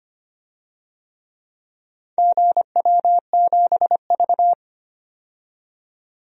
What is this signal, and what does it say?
Morse code, heard as a keyed tone, about 25 wpm, 700 Hz. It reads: GW7V
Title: DXCC-100 Call Signs - 4 Characters - 25wpm